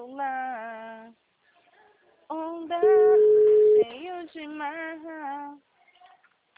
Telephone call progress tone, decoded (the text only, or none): ringback tone